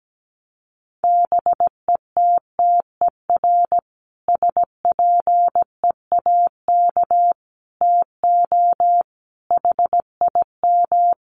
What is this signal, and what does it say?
Morse code, heard as a keyed tone, about 17 wpm, 700 Hz. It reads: BETTER SPEAK TO HIM
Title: Sentences from Top 1100 Words 17wpm